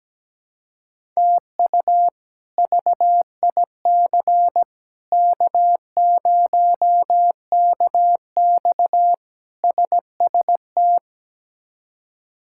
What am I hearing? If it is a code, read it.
Morse code, heard as a keyed tone, about 17 wpm, 700 Hz. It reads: TU VIC K0KX SST
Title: K1USN SST QSO Practice - 17wpm